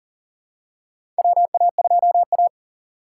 Morse code, heard as a keyed tone, about 40 wpm, 700 Hz. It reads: WA2A
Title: USA Call Signs - 2x Character Spacing - 40wpm